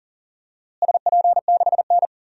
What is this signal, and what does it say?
Morse code, heard as a keyed tone, about 40 wpm, 700 Hz. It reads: SP6N
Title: DXCC-100 Call Signs - 4 Characters - Repeated 2x - 40wpm